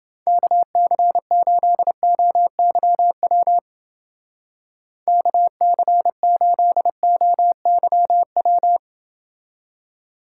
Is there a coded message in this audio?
Yes: Morse code from 0.3 to 8.8 s, heard as a keyed tone, about 29 wpm, 700 Hz. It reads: KC8OYW KC8OYW